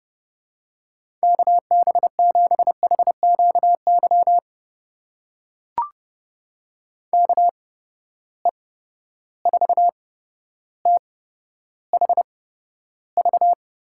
Morse code, heard as a keyed tone, about 30 wpm, spaced as for 13 wpm, 700 Hz. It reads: KB7HQY KE4THV
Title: USA Call Signs - 6 Characters - 8x Character Spacing - 30wpm